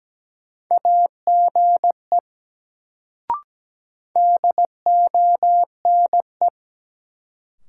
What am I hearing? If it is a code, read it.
Morse code, heard as a keyed tone, about 17 wpm, 700 Hz. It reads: AGE DONE
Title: Sentences from Top 500 Words - New Word Drill - 17wpm